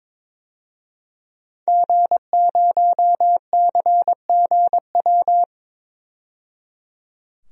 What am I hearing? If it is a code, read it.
Morse code, heard as a keyed tone, about 22 wpm, 700 Hz. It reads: G0CGW
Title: DXCC-100 Call Signs - 5 Characters - 22wpm